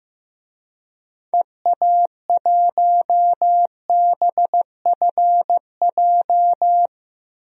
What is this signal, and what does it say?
Morse code, heard as a keyed tone, about 15 wpm, 700 Hz. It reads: EA1BFJ
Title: DXCC-100 Call Signs - 6 Characters - 15wpm